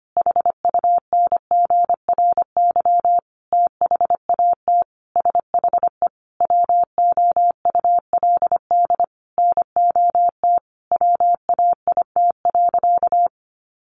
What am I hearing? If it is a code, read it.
Morse code, heard as a keyed tone, about 25 wpm, 700 Hz. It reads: HUNGRY THAT SHE WOULD NOT WAIT.
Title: Sentences from Top 1400 Words 25wpm